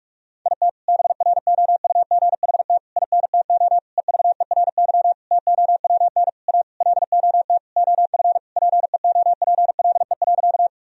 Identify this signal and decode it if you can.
Morse code, heard as a keyed tone, about 46 wpm, 700 Hz. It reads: IT BROUGHT INTO EVERY TOWN A LOT OF PEOPLE.
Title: Sentences from Top 1000 Words - Review Part 3 - 45wpm